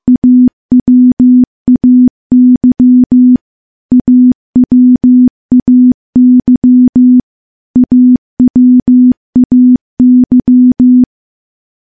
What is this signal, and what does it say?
Morse code, heard as a keyed tone, about 16 wpm, 250 Hz. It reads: AWAY AWAY AWAY
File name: morse / morse-training-15WPM